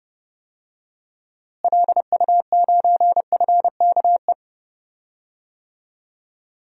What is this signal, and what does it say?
Morse code, heard as a keyed tone, about 31 wpm, 700 Hz. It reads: LU9FKE
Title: DXCC-100 Call Signs - 6 Characters - Repeated 3x - 30wpm